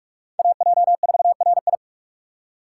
Morse code, heard as a keyed tone, about 44 wpm, 700 Hz. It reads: AJ4RI